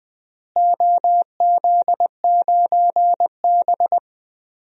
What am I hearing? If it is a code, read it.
Morse code, heard as a keyed tone, about 20 wpm, 700 Hz. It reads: OZ9B
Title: DXCC-100 Call Signs - 4 Characters - Repeated 2x - 20wpm